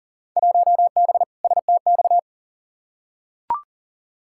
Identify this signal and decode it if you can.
Morse code, heard as a keyed tone, about 40 wpm, 700 Hz. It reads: 1B STX
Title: Field Day Exchanges Encore - 40wpm